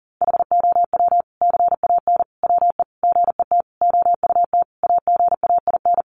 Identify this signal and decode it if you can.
Morse code, heard as a keyed tone, about 40 wpm, 700 Hz. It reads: HOW CAN WE GET OUT AGAIN
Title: Sentences from Top 1200 Words 40wpm